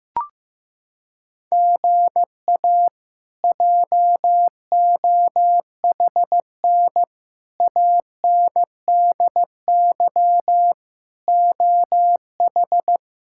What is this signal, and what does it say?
Morse code, heard as a keyed tone, about 15 wpm, 700 Hz. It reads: GA JOHN ANDY OH
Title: K1USN SST QSO Practice - 15wpm